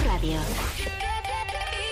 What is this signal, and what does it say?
Speech. There is no code.